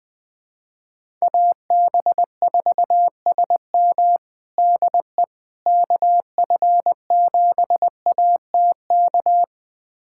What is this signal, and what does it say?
Morse code, heard as a keyed tone, about 20 wpm, 700 Hz. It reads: AB4SM DE KF7ATK
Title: USA Callsign DE Callsign 20wpm